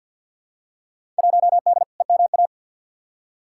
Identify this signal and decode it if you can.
Morse code, heard as a keyed tone, about 49 wpm, 700 Hz. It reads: 1D EMA
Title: Field Day Exchanges - 50wpm